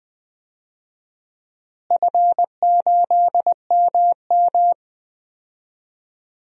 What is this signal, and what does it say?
Morse code, heard as a keyed tone, about 21 wpm, 700 Hz. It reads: F8MM
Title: DXCC-100 Call Signs - 4 Characters - Repeated 2x - 20wpm